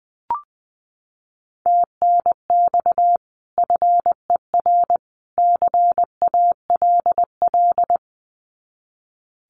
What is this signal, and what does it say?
Morse code, heard as a keyed tone, about 20 wpm, 700 Hz. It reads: TNX FER CALL
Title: Common QSO Phrases 20wpm